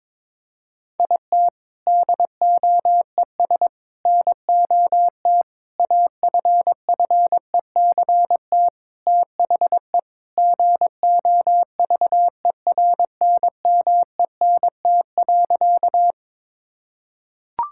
Morse code, heard as a keyed tone, about 22 wpm, 700 Hz. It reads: IT DOES NOT AFFECT THE GOVERNMENT.
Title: Sentences from Top 1000 Words - Review Part 6 - 22wpm